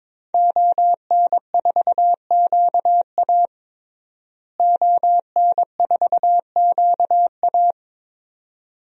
Morse code, heard as a keyed tone, about 22 wpm, 700 Hz. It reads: ON4QA ON4QA
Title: DXCC-100 Call Signs - 5 Characters - Repeated 2x - 22wpm